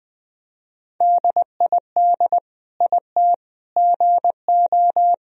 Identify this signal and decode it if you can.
Morse code, heard as a keyed tone, about 20 wpm, 700 Hz. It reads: DID IT GO
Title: Sentences from Top 1000 Words - Review Part 1 - 20wpm